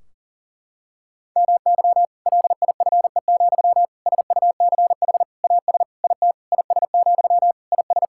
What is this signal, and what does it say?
Morse code, heard as a keyed tone, about 40 wpm, 700 Hz. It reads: MY LIFE, SUCH AS IT IS, IS
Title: Sentences from Top 1600 Words 40wpm